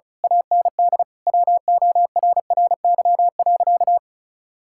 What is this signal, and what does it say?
Morse code, heard as a keyed tone, about 36 wpm, 700 Hz. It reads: AND WORRY.
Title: Pangram Challenge v2 - 35wpm